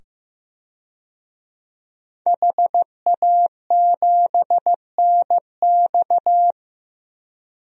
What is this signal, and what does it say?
Morse code, heard as a keyed tone, about 15 wpm, 700 Hz. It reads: HA7NX
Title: DXCC-100 Call Signs - 5 Characters - 15wpm